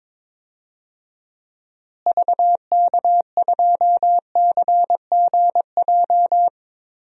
Morse code, heard as a keyed tone, about 21 wpm, 700 Hz. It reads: VK2CGJ
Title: DXCC-100 Call Signs - 6 Characters - Repeated 3x - 22wpm